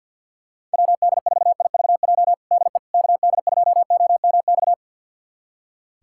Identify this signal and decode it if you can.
Morse code, heard as a keyed tone, about 47 wpm, 700 Hz. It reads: WD4IVJ DE KD2OMX
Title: USA Callsign DE Callsign 50wpm